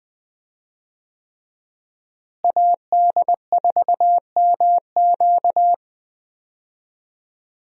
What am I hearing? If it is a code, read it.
Morse code, heard as a keyed tone, about 20 wpm, 700 Hz. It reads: AD4MQ
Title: USA Call Signs - 5 Characters - Repeated 3x - 20wpm